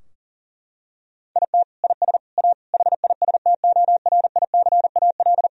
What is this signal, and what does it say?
Morse code, heard as a keyed tone, about 40 wpm, 700 Hz. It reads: IT IS A HISTORICAL
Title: Sentences from Top 1000 Words - Review Part 4 - 40wpm